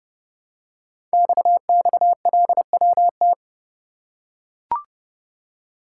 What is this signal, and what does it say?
Morse code, heard as a keyed tone, about 29 wpm, 700 Hz. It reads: XXLWT
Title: Code Groups - 5 Letters - 30wpm